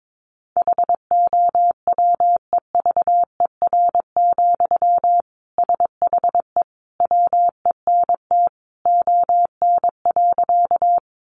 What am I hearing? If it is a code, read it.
Morse code, heard as a keyed tone, about 22 wpm, 700 Hz. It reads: HOWEVER, SHE WENT ON.
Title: Sentences from Top 600 Words 22wpm